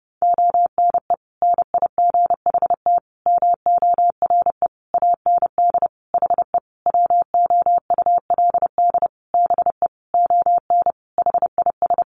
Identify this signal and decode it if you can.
Morse code, heard as a keyed tone, about 30 wpm, 700 Hz. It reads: ONE NIGHT MORE AND HE WOULD BE ON HIS
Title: Sentences from Top 300 Words 30wpm